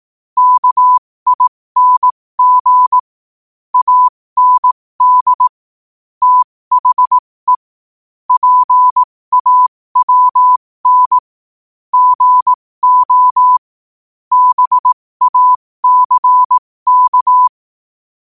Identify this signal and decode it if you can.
Morse code, heard as a keyed tone, about 18 wpm, spaced as for 15 wpm, 1000 Hz. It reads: KING AND THE PAWN GO BACK
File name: qotdfiles / 15wpm